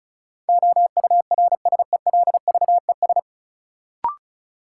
Morse code, heard as a keyed tone, about 36 wpm, 700 Hz. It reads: OURSELVES
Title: Sentences from Top 1600 Words - New Word Drill - 35wpm